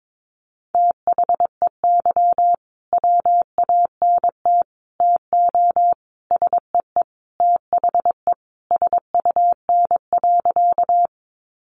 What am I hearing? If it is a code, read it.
Morse code, heard as a keyed tone, about 22 wpm, 700 Hz. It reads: THEY WANT TO SEE THE SUN.
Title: Easy Phrases 22wpm